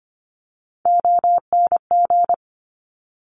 Morse code, heard as a keyed tone, about 25 wpm, 700 Hz. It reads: ONG
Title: Top Three Letter Combinations in Words 25wpm